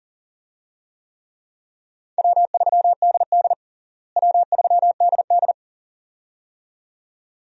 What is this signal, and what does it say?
Morse code, heard as a keyed tone, about 40 wpm, 700 Hz. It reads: W3DD W3DD
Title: USA Call Signs - 4 Characters - Repeated 2x - 40wpm